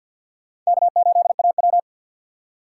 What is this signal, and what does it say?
Morse code, heard as a keyed tone, about 58 wpm, 700 Hz. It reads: K8AW
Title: Speed Racing - USA 1x2 Call Signs 50wpm